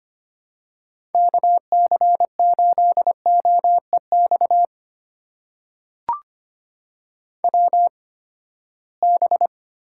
Morse code, heard as a keyed tone, about 25 wpm, spaced as for 17 wpm, 700 Hz. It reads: KC8OEX WB
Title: USA Call Signs - 6 Characters - 8x Character Spacing - 25wpm